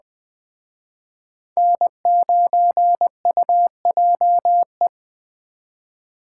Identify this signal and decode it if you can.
Morse code, heard as a keyed tone, about 21 wpm, 700 Hz. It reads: N9UJE
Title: Speed Racing - USA 1x3 Call Signs 17wpm